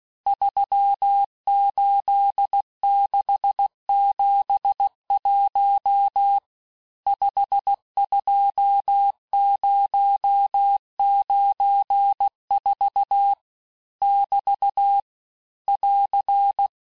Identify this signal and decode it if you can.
Morse code, heard as a keyed tone, about 16 wpm, 800 Hz. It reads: 38671 52094 <BT> <AR>
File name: cours-du-lundi files / cours-1592--teste-chiffre-vitesse-15mpm